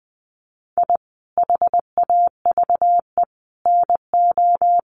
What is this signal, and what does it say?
Morse code, heard as a keyed tone, about 20 wpm, 700 Hz. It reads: I HAVE NO
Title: Sentences from Top 1100 Words 20wpm